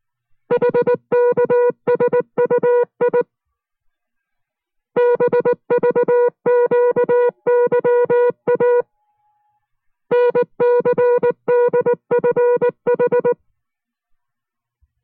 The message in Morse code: HKSUI BVQYA NCDFH